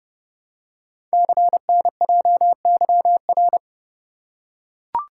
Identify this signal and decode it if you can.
Morse code, heard as a keyed tone, about 28 wpm, 700 Hz. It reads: CNJYR
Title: Code Groups - 5 Letters - 30wpm